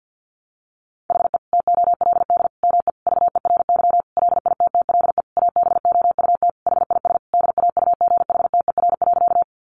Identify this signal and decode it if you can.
Morse code, heard as a keyed tone, about 50 wpm, 700 Hz. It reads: HE TOLD ME VERY LITTLE ABOUT HIS DAUGHTER.
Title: Sentences from Top 1500 Words 50wpm